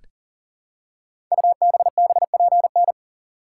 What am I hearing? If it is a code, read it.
Morse code, heard as a keyed tone, about 38 wpm, 700 Hz. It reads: UBBPN
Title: Code Groups - 5 Letter:Numbers - 40wpm